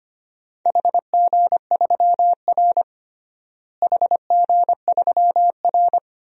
Morse code, heard as a keyed tone, about 25 wpm, 700 Hz. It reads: HG3R HG3R
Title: DXCC-100 Call Signs - 4 Characters - Repeated 2x - 25wpm